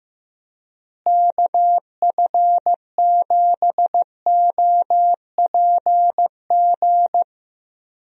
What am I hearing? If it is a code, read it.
Morse code, heard as a keyed tone, about 15 wpm, 700 Hz. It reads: KF7OPG
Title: USA Call Signs - 6 Characters - Repeated 3x - 15wpm